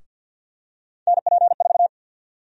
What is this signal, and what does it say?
Morse code, heard as a keyed tone, about 49 wpm, 700 Hz. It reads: NP4
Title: USA Call Sign Prefixes - Repeated 2x - 50wpm